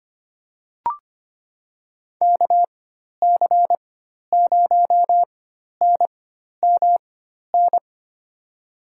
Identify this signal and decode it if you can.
Morse code, heard as a keyed tone, about 25 wpm, spaced as for 12 wpm, 700 Hz. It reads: KC0NMN
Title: USA Call Signs - 6 Characters - 4x Character Spacing - 25wpm